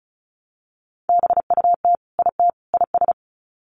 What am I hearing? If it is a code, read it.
Morse code, heard as a keyed tone, about 34 wpm, 700 Hz. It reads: BUT IT IS
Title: Sets of 3 Words 35wpm